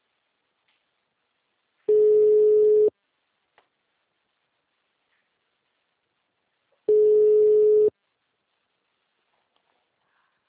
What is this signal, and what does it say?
Ringback tone (Europe, 425 Hz).
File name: out- -2006- - - .47